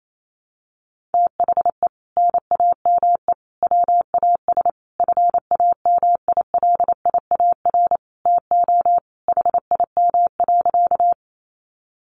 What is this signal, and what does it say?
Morse code, heard as a keyed tone, about 28 wpm, 700 Hz. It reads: THE NAME WAS FAMILIAR TO HIM.
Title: Sentences from Top 1000 Words 28wpm